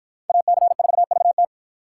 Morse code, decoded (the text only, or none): AC4VT